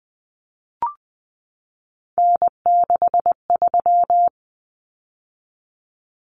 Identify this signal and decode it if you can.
Morse code, heard as a keyed tone, about 20 wpm, 700 Hz. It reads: N63
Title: Common Cut Numbers - 3-Digit 20wpm